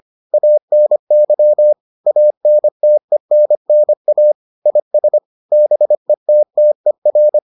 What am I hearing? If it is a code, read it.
Morse code, heard as a keyed tone, about 25 wpm, 600 Hz. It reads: ANY ANTENNA IS BETTER